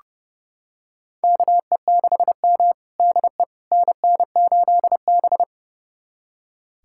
Morse code, heard as a keyed tone, about 30 wpm, 700 Hz. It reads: KE6M DE NN8B
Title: USA Callsign DE Callsign - 4 Characters - 30wpm